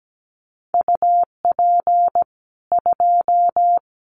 Morse code, heard as a keyed tone, about 17 wpm, 700 Hz. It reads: UP 2
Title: Common QSO Phrases 17wpm